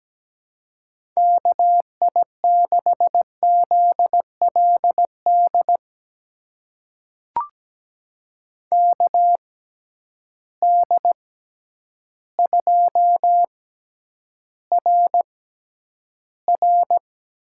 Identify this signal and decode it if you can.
Morse code, heard as a keyed tone, about 17 wpm, spaced as for 9 wpm, 700 Hz. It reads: KI6ZLD KD2RR
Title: USA Call Signs - 6x Character Spacing - 17wpm